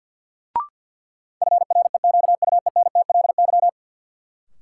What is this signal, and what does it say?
Morse code, heard as a keyed tone, about 50 wpm, 700 Hz. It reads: FREQUENTLY
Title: Sentences from Top 1300 Words - New Word Drill Alt - 50wpm